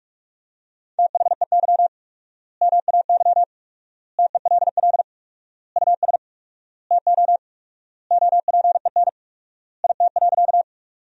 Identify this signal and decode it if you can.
Morse code, heard as a keyed tone, about 46 wpm, 700 Hz. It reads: THEY MAY TELL US TO OPEN IT.
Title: 4x Word-Spacing - Sentences from Top 300 Words - 45wpm